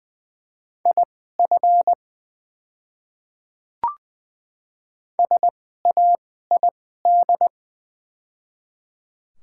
Morse code, heard as a keyed tone, about 20 wpm, spaced as for 15 wpm, 690 Hz. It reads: IF SAID
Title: 2x Character Spacing - Top 100 Words - 20wpm